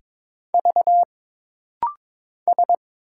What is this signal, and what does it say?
Morse code, heard as a keyed tone, about 22 wpm, 700 Hz. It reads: V S
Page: This practice set is composed of Letters, Numbers, common punctuations, and common prosigns.